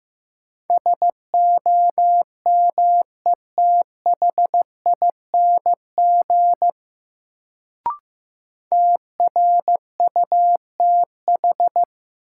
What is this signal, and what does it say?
Morse code, heard as a keyed tone, about 15 wpm, 700 Hz. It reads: SOMETHING TRUTH
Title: Sets of 5 Words - New Word Drill - 15wpm